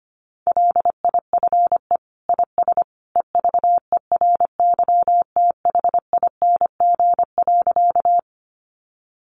Morse code, heard as a keyed tone, about 25 wpm, 700 Hz. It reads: LIFE IS EVERYTHING.
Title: Sentences from Top 700 Words 25wpm